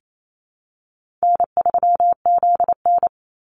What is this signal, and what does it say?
Morse code, heard as a keyed tone, about 28 wpm, 700 Hz. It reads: N3ZN
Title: Easy Call Signs Repeated 2x - 28wpm